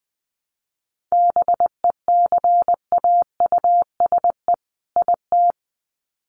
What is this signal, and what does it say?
Morse code, heard as a keyed tone, about 20 wpm, 700 Hz. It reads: BECAUSE IT
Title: Sets of 2 Words 20wpm